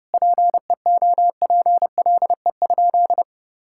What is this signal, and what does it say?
Morse code, heard as a keyed tone, about 30 wpm, 700 Hz. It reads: PEOPLE?
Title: Sentences from Top 1000 Words - Review Part 6 - 30wpm